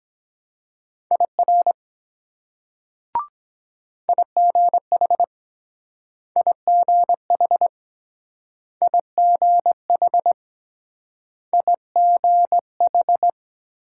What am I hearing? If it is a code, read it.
Morse code, heard as a keyed tone, about 23 wpm, 700 Hz. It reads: IR IGH IGH IGH IGH